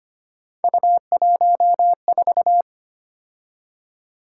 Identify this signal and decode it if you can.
Morse code, heard as a keyed tone, about 25 wpm, 700 Hz. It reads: U14
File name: Three Letter-Number Combinations - 25wpm